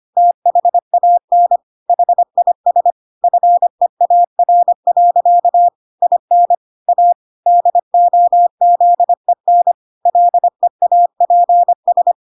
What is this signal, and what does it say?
Morse code, heard as a keyed tone, about 25 wpm, 700 Hz. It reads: THAN HIS FEAR. IN A DOZEN LEAPS